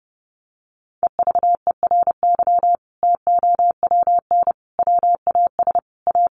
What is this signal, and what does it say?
Morse code, heard as a keyed tone, about 28 wpm, 700 Hz. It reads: EVERY TOWN WAS A
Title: Sentences from Top 1500 Words 30wpm